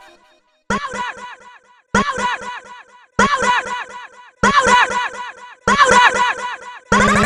There is no code.